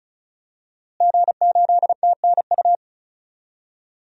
Morse code, heard as a keyed tone, about 36 wpm, 700 Hz. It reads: G8TNU
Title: DXCC-100 Call Signs - 5 Characters - Repeated 3x - 35wpm